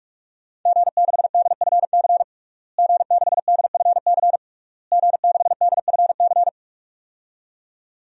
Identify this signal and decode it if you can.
Morse code, heard as a keyed tone, about 46 wpm, 700 Hz. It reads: G6DFC G6DFC G6DFC
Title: DXCC-100 Call Signs - 5 Characters - Repeated 3x - 45wpm